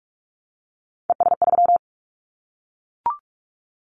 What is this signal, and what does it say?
Morse code, heard as a keyed tone, about 44 wpm, 700 Hz. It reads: ES3
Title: Callsign Prefixes for Top 50 DXCC - 45wpm